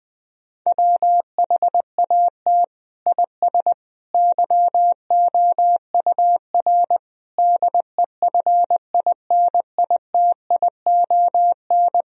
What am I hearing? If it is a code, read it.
Morse code, heard as a keyed tone, about 20 wpm, 700 Hz. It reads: WHAT IS YOUR DEFINITION
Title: Sentences from Top 1000 Words - Review Part 4 - 20wpm